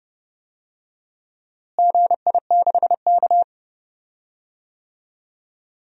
Morse code, heard as a keyed tone, about 30 wpm, 700 Hz. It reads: GI6K
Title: DXCC-100 Call Signs - 4 Characters - Repeated 2x - 30wpm